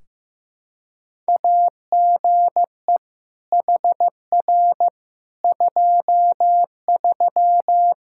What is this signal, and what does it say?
Morse code, heard as a keyed tone, about 15 wpm, 700 Hz. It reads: AGE HR 23
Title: Practice QSO - Exchange 3 - More Info - 15wpm